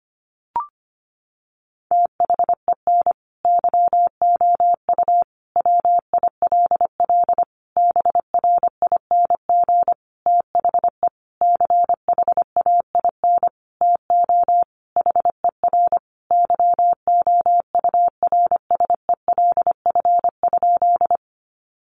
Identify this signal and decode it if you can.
Morse code, heard as a keyed tone, about 25 wpm, 700 Hz. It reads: THEN YOU WILL BRING THE CHAIN TO HER YOURSELF?
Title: Sentences from Top 1500 Words 25wpm